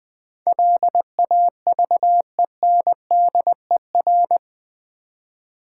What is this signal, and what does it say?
Morse code, heard as a keyed tone, about 20 wpm, 700 Hz. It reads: LAVENDER
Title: Difficult Words with Hints - 20wpm